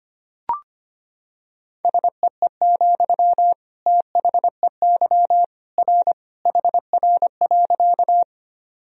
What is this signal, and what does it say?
Morse code, heard as a keyed tone, about 25 wpm, 700 Hz. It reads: SEE, THEY R HR.
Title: Sentences from Top 200 Words and Abbreviations 25wpm